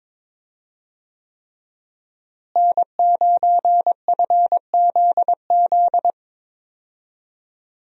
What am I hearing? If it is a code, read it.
Morse code, heard as a keyed tone, about 22 wpm, 700 Hz. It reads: N9FZZ
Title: USA Call Signs - 5 Characters - Repeated 3x - 22wpm